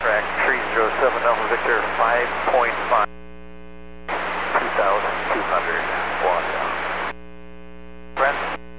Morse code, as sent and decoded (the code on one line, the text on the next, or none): none